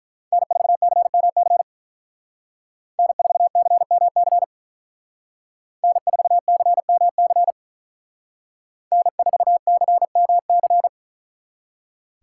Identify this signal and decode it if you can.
Morse code, heard as a keyed tone, about 44 wpm, 700 Hz. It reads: N4CMC N4CMC N4CMC N4CMC
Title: Speed Racing - USA 1x3 Call Signs 35wpm